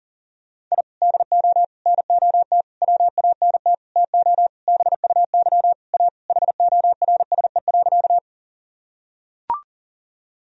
Morse code, heard as a keyed tone, about 40 wpm, 700 Hz. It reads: I DO NOT WANT TO BUY A HORSE.
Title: Sentences from Top 1000 Words - Review Part 4 - 40wpm